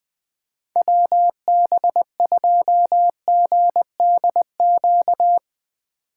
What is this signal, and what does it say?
Morse code, heard as a keyed tone, about 20 wpm, 700 Hz. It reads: WB2GDQ
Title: USA Call Signs - 6 Characters - 4x Character Spacing - 20wpm